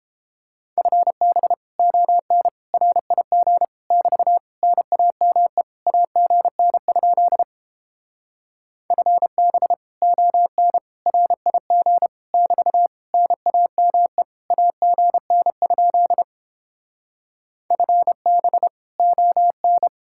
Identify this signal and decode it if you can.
Morse code, heard as a keyed tone, about 29 wpm, 700 Hz. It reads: FB ON RIG <BT> NAME AGN? FB ON RIG <BT> NAME AGN? FB ON